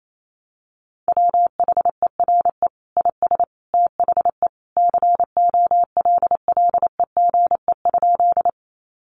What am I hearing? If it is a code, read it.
Morse code, heard as a keyed tone, about 28 wpm, 700 Hz. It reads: WHERE IS THE COLLEGE?
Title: Sentences from Top 1300 Words 28wpm